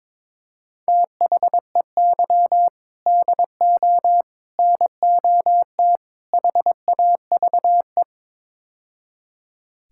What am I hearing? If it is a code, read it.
Morse code, heard as a keyed tone, about 22 wpm, 700 Hz. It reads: THEY DO NOT HAVE
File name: practice / 22wpm / Build A Phrase - 5-Word Phrases - 22wpm